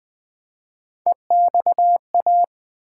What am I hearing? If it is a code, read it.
Morse code, heard as a keyed tone, about 20 wpm, 700 Hz. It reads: EXA
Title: Build A Word - 1 Letter At A Time - 4-Letter Words - 20wpm